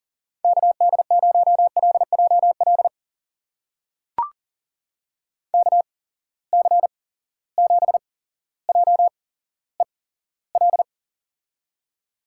Morse code, heard as a keyed tone, about 40 wpm, spaced as for 17 wpm, 700 Hz. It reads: KD0LJL KC7JEL
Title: USA Call Signs - 6 Characters - 8x Character Spacing - 40wpm